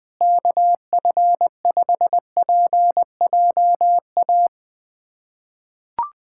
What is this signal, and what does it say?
Morse code, heard as a keyed tone, about 21 wpm, 700 Hz. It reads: KF5PJA
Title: USA Callsign DE Callsign - 6 Characters - 20wpm